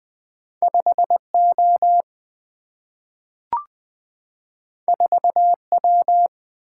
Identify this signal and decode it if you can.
Morse code, heard as a keyed tone, about 21 wpm, 700 Hz. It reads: 5O 4W
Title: Single-Letter Single-Number 20wpm